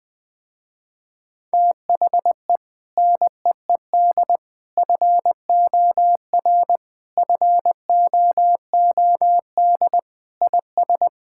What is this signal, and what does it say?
Morse code, heard as a keyed tone, about 21 wpm, 700 Hz. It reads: THE NEED FOR FOOD IS
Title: Sentences with 3-Word Phrases - Spoken Beginning - 20wpm